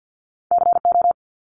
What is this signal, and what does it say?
Morse code, heard as a keyed tone, about 50 wpm, 700 Hz. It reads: CO
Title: ARRL Sections 50wpm